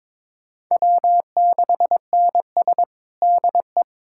Morse code, heard as a keyed tone, about 22 wpm, 700 Hz. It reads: W6NS DE
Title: Practice QSO - Exchange 4 - Ending - 22wpm